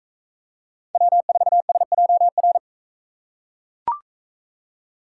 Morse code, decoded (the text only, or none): W4SJR